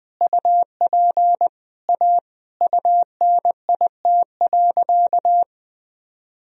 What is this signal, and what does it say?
Morse code, heard as a keyed tone, about 20 wpm, 700 Hz. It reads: UP A UNIT.
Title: Sentences from Top 1000 Words - Review Part 2 - 20wpm